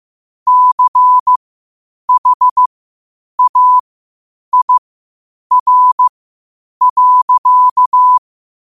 Morse code, heard as a keyed tone, about 15 wpm, spaced as for 8 wpm, 1000 Hz. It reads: CHAIR.
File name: qotdfiles / 8wpm